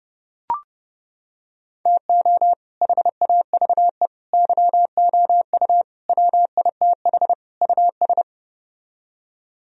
Morse code, heard as a keyed tone, about 31 wpm, 700 Hz. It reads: TO HAVE YOU WITH US